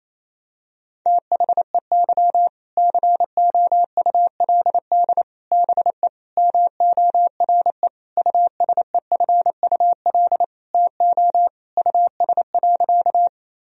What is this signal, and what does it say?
Morse code, heard as a keyed tone, about 28 wpm, 700 Hz. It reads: THEY COULD BE MORE USEFUL TO US.
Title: Sentences from Top 1000 Words - Review Part 4 - 28wpm